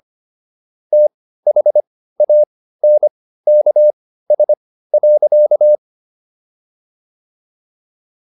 Morse code, heard as a keyed tone, about 25 wpm, spaced as for 15 wpm, 600 Hz. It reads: THANKS.